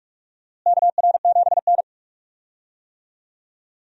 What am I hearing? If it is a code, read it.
Morse code, heard as a keyed tone, about 42 wpm, 700 Hz. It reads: KR7N
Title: USA Call Signs - 4 Characters - Repeated 2x - 45wpm